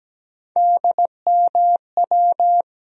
Morse code, heard as a keyed tone, about 17 wpm, 700 Hz. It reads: DMW
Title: USA Call Sign Postfixes - 3 Letters 17wpm